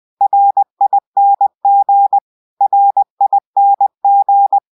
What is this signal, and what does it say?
Morse code, heard as a keyed tone, about 20 wpm, 800 Hz. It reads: RING RING
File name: go-download